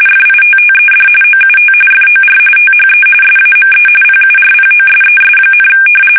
RTTY, about 45.45 baud. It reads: FITTED OUT FOR THE SAME PURPOSES,
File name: fsk850 / usnavy1